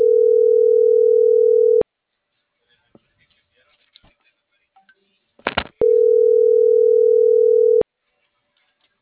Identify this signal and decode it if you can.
Ringback tone (North America), 440 + 480 Hz.